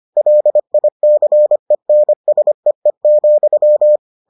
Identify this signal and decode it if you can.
Morse code, heard as a keyed tone, about 25 wpm, 600 Hz. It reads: LICENSEE,